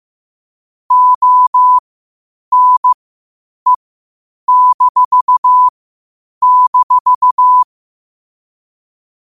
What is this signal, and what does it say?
Morse code, heard as a keyed tone, about 15 wpm, spaced as for 8 wpm, 1000 Hz. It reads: ONE--